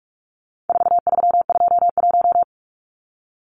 Morse code, heard as a keyed tone, about 44 wpm, 700 Hz. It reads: 4321
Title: Numbers - 4-Digit 45wpm